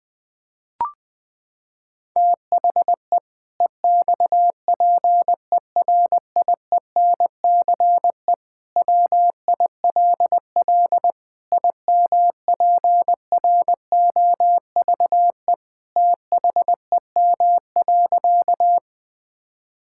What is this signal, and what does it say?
Morse code, heard as a keyed tone, about 20 wpm, 700 Hz. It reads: THE EXPERIENCE WILL IMPROVE THEM.
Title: Sentences from Top 1000 Words - Review Part 5 - 20wpm